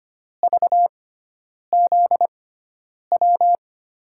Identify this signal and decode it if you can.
Morse code, heard as a keyed tone, about 25 wpm, spaced as for 9 wpm, 700 Hz. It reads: VZW
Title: USA Call Signs - 5 Characters - 6x Character Spacing - 25wpm